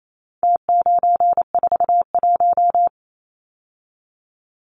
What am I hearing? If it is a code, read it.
Morse code, heard as a keyed tone, about 28 wpm, 700 Hz. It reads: T941
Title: Numbers - Contest Style - 4-Digit 28wpm